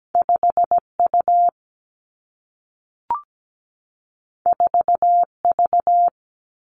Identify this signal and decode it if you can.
Morse code, heard as a keyed tone, about 17 wpm, 700 Hz. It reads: 5U 4V
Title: Two Challenging Letter-Number Combinations 17wpm